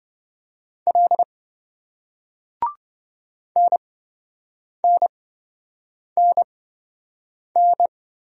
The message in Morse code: L NNNN

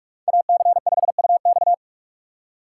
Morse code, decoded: AC5UX

